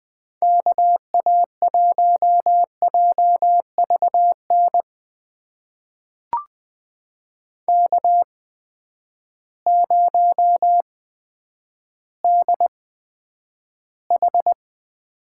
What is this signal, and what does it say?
Morse code, heard as a keyed tone, about 20 wpm, spaced as for 10 wpm, 700 Hz. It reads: KA1JVN K0DH
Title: USA Call Signs - 8x Character Spacing - 20wpm